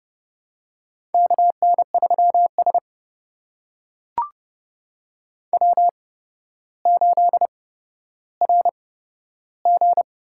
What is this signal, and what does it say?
Morse code, heard as a keyed tone, about 30 wpm, spaced as for 13 wpm, 700 Hz. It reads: KN3S W8RG
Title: USA Call Signs - 4 Characters - 8x Character Spacing - 30wpm